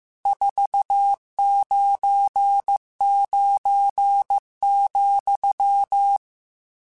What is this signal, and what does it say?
Morse code, heard as a keyed tone, about 15 wpm, 800 Hz. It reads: 499,
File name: RSTs / RST-mp3s / Various RST @13